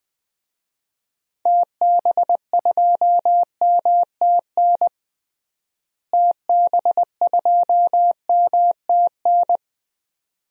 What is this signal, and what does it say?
Morse code, heard as a keyed tone, about 21 wpm, 690 Hz. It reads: TB2MTN TB2MTN
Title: DXCC-100 Call Signs - 6 Characters - Repeated 2x - 20wpm